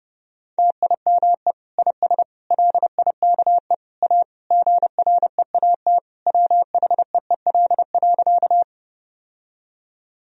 Morse code, heard as a keyed tone, about 30 wpm, 700 Hz. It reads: TIME IS LIKE A GREAT WHEEL.
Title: Sentences from Top 1000 Words - Review Part 3 - 30wpm